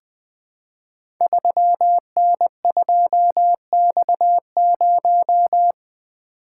Morse code, heard as a keyed tone, about 20 wpm, 700 Hz. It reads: 3N2X0
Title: Code Groups - 5 Letter:Numbers - 20wpm